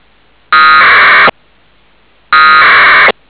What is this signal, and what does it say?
Packet radio (AX.25, as in APRS).